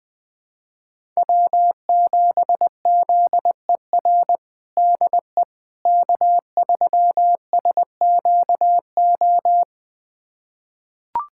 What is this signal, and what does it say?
Morse code, heard as a keyed tone, about 21 wpm, 700 Hz. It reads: W7ZER DE K3SQO
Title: USA Callsign DE Callsign - 5 Characters - 20wpm